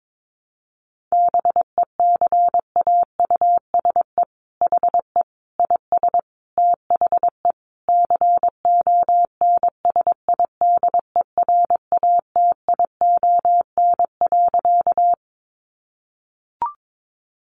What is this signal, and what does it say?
Morse code, heard as a keyed tone, about 22 wpm, 700 Hz. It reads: BECAUSE HE IS THE CONSIDERATION.
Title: Sentences from Top 1500 Words 22wpm